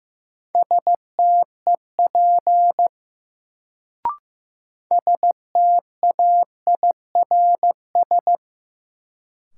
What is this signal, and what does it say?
Morse code, heard as a keyed tone, about 15 wpm, 700 Hz. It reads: STEP STAIRS
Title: Sets of 5 Words Encore - New Word Drill - 15wpm